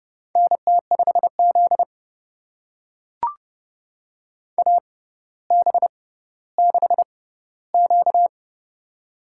Morse code, heard as a keyed tone, about 30 wpm, spaced as for 15 wpm, 700 Hz. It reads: NT5Z AB6Q
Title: USA Call Signs - 4 Characters - 6x Character Spacing - 30wpm